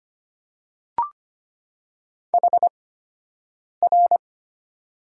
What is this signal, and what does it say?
Morse code, heard as a keyed tone, about 25 wpm, spaced as for 7 wpm, 700 Hz. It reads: HR
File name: Top 90 QSO Elements - 8x Character Spacing - 25wpm